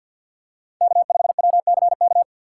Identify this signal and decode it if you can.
Morse code, heard as a keyed tone, about 47 wpm, 690 Hz. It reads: K5WCK
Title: USA Call Signs - 2x Character Spacing - 50wpm